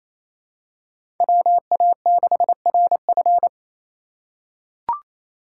Morse code, heard as a keyed tone, about 28 wpm, 700 Hz. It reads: WA6RF
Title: USA Call Signs - 5 Characters - 2x Character Spacing - 28wpm